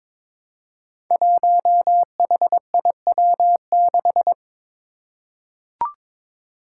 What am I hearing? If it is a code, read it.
Morse code, heard as a keyed tone, about 21 wpm, 700 Hz. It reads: 1HIW6
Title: Five Challenging Letter-Number Combinations 22wpm